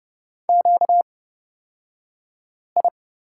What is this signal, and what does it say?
Morse code, heard as a keyed tone, about 30 wpm, spaced as for 5 wpm, 700 Hz. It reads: QI
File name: Lesson 38d - Letter q - Two Characters 30wpm